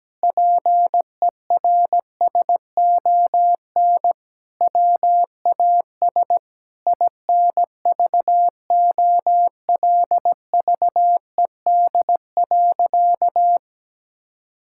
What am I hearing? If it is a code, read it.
Morse code, heard as a keyed tone, about 17 wpm, 700 Hz. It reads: PERSON WAS INVOLVED.
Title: Sentences from Top 1000 Words - Review Part 5 - 17wpm